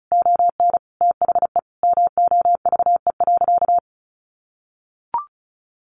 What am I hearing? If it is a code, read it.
Morse code, heard as a keyed tone, about 34 wpm, 700 Hz. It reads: ON THE MOVE.
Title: Sentences from Top 1400 Words 35wpm